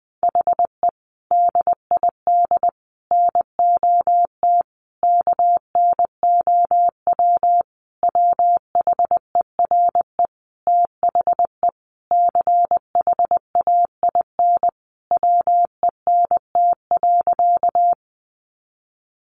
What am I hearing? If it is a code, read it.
Morse code, heard as a keyed tone, about 19 wpm, 700 Hz. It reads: HE DID NOT KNOW WHERE THE CHAIN WENT.
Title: Sentences from Top 1500 Words 20wpm